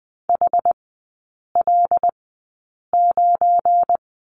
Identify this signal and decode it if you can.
Morse code, heard as a keyed tone, about 20 wpm, spaced as for 8 wpm, 700 Hz. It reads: HL9
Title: 218 Lesson 35e - Number 0 - Three Characters 20wpm